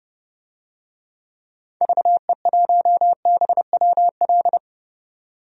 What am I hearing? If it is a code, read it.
Morse code, heard as a keyed tone, about 30 wpm, 700 Hz. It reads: VE1BWL
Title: DXCC-100 Call Signs - 6 Characters - 30wpm